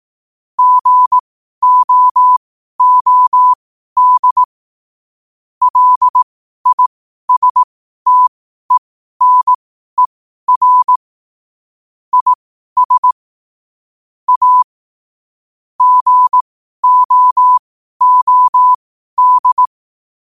Morse code, heard as a keyed tone, about 18 wpm, spaced as for 12 wpm, 1000 Hz. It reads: GOOD LISTENER IS A GOOD
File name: qotdfiles / 12wpm